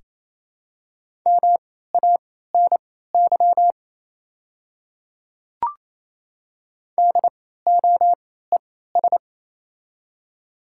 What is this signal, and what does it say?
Morse code, heard as a keyed tone, about 28 wpm, spaced as for 16 wpm, 700 Hz. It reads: MANY DOES
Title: 3x Character Spacing - Top 200 Words - 28wpm